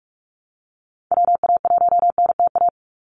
Morse code, heard as a keyed tone, about 44 wpm, 700 Hz. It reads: WA1NTA
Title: Call Signs Repeated 2x - 45wpm